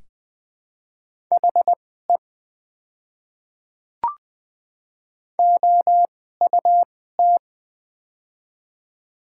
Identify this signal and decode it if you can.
Morse code, heard as a keyed tone, about 20 wpm, spaced as for 14 wpm, 700 Hz. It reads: HE OUT
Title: 2x Character Spacing - Top 100 Words - 20wpm